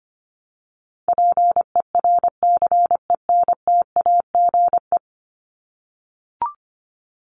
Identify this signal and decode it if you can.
Morse code, heard as a keyed tone, about 25 wpm, 700 Hz. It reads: PERCENTAGE
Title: Top 1001-1250 Words 25wpm